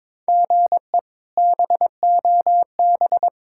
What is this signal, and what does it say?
Morse code, heard as a keyed tone, about 21 wpm, 700 Hz. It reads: GE BOB
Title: K1USN SST QSO Practice - 22wpm